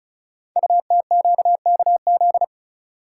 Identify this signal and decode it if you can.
Morse code, heard as a keyed tone, about 36 wpm, 690 Hz. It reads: UTQKZ
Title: Code Groups - 5 Letters - 35wpm